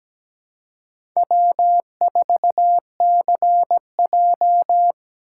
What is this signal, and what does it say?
Morse code, heard as a keyed tone, about 17 wpm, 700 Hz. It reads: W4CJ
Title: USA Call Signs - 4 Characters - 2x Character Spacing - 17wpm